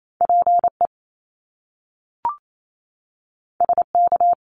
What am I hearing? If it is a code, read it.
Morse code, heard as a keyed tone, about 28 wpm, 700 Hz. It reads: PE SK
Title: Canadian Provinces and Territories 28wpm